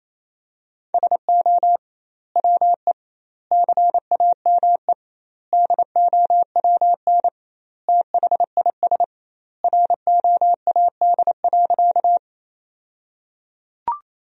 Morse code, heard as a keyed tone, about 28 wpm, 700 Hz. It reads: SO WE CAME DOWN THIS ROAD.
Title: 2x Word-Spacing - Sentences from Top 400 Words - 28wpm